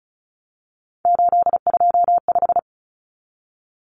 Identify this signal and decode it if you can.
Morse code, heard as a keyed tone, about 34 wpm, 700 Hz. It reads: 825
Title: Common Cut Numbers - 3-Digit 35wpm